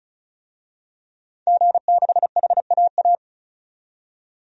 Morse code, heard as a keyed tone, about 34 wpm, 700 Hz. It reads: G6HAA
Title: DXCC-100 Call Signs - 5 Characters - 35wpm